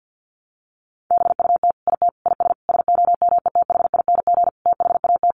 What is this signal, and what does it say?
Morse code, heard as a keyed tone, about 50 wpm, 700 Hz. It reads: BUT IT IS SOMETHING THAT
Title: Sentences from Top 1000 Words 50wpm